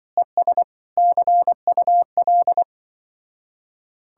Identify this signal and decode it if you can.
Morse code, heard as a keyed tone, about 24 wpm, 700 Hz. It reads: ES CUL
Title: Speed-Racing Common QSO Phrases - 20wpm